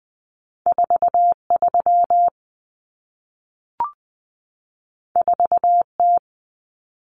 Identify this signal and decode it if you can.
Morse code, heard as a keyed tone, about 20 wpm, 700 Hz. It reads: 43 4T
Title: Common Cut Numbers - 2-Digit 20wpm